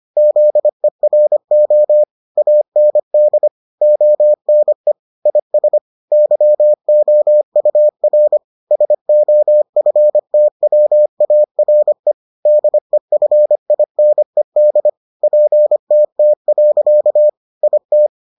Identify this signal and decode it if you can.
Morse code, heard as a keyed tone, about 25 wpm, 600 Hz. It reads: ZERO AND ONE IS YOUR SOFTWARE DEFINED PTT. IT